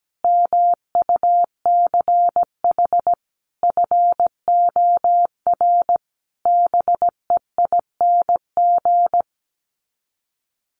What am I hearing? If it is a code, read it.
Morse code, heard as a keyed tone, about 17 wpm, 700 Hz. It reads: MUCH FOR BEING
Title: Sets of 5 Words Encore 17wpm